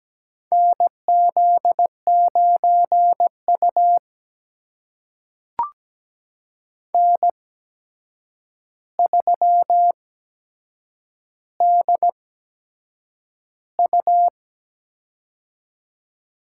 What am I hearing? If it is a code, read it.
Morse code, heard as a keyed tone, about 17 wpm, spaced as for 7 wpm, 700 Hz. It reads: NZ9U N3DU
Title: USA Call Signs - 4 Characters - 8x Character Spacing - 17wpm